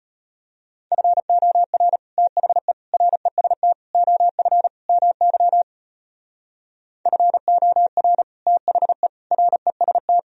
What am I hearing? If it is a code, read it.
Morse code, heard as a keyed tone, about 33 wpm, 700 Hz. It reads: FOR THE REST OF MY FOR THE REST